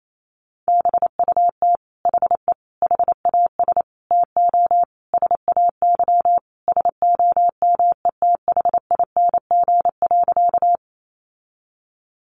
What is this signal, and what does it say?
Morse code, heard as a keyed tone, about 28 wpm, 700 Hz. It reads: BUT HE HAS TO SAY SOMETHING.
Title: Sentences from Top 600 Words 28wpm